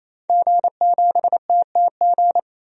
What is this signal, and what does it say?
Morse code, heard as a keyed tone, about 27 wpm, 700 Hz. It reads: G7TTG
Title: DXCC-100 Call Signs - 5 Characters - 28wpm